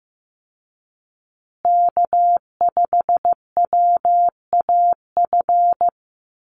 Morse code, heard as a keyed tone, about 14 wpm, 700 Hz. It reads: K5WAF
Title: Call Signs Repeated 2x - 15wpm